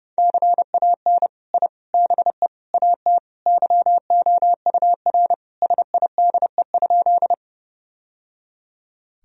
Morse code, heard as a keyed tone, about 30 wpm, 700 Hz. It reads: CAN I BE AT YOUR SIDE?
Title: Sentences from Top 1000 Words - Review Part 1 - 30wpm